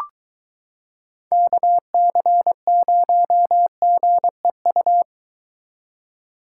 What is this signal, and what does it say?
Morse code, heard as a keyed tone, about 23 wpm, 700 Hz. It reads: KC0GEU